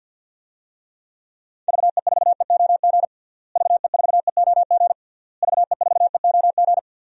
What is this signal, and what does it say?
Morse code, heard as a keyed tone, about 51 wpm, 700 Hz. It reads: VE4EOG VE4EOG VE4EOG
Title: DXCC-100 Call Signs - 6 Characters - Repeated 3x - 50wpm